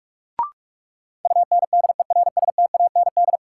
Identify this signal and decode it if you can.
Morse code, heard as a keyed tone, about 46 wpm, 700 Hz. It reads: UNDERSTAND
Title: Sets of 4 Words - New Word Drill - 45wpm